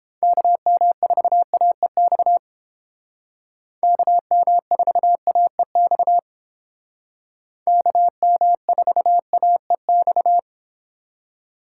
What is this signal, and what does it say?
Morse code, heard as a keyed tone, about 30 wpm, 700 Hz. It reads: KM4AEX KM4AEX KM4AEX